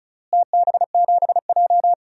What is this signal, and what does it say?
Morse code, heard as a keyed tone, about 36 wpm, 700 Hz. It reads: TB7J
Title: DXCC-100 Call Signs - 4 Characters - Repeated 2x - 35wpm